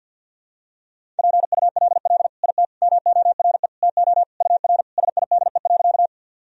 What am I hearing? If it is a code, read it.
Morse code, heard as a keyed tone, about 48 wpm, 700 Hz. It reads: PULL IT MORE TO UR SIDE.
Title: Sentences from Top 400 Words and Abbreviations 50wpm